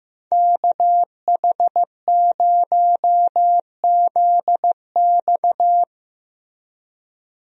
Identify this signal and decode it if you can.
Morse code, heard as a keyed tone, about 16 wpm, 700 Hz. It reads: KH0ZX
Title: USA Call Signs - 5 Characters - Repeated 3x - 15wpm